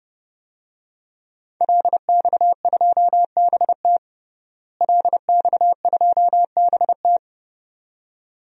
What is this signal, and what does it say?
Morse code, heard as a keyed tone, about 32 wpm, 700 Hz. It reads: LX2BT LX2BT
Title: DXCC-100 Call Signs - 5 Characters - Repeated 2x - 30wpm